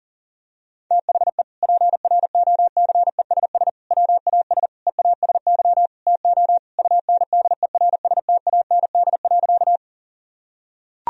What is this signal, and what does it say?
Morse code, heard as a keyed tone, about 40 wpm, 700 Hz. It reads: THE PROCESS WAS EASY TO UNDERSTAND.
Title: Sentences from Top 1000 Words - Review Part 3 - 40wpm